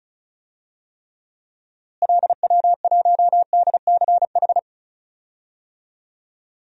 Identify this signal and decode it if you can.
Morse code, heard as a keyed tone, about 36 wpm, 700 Hz. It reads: LW1DCH
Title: DXCC-100 Call Signs - 6 Characters - Repeated 2x - 35wpm